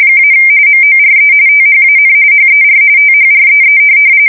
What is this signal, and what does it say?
RTTY, about 45.45 baud. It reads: - MORE THAN HER THREE
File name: fsk170 / USS-New-Jersey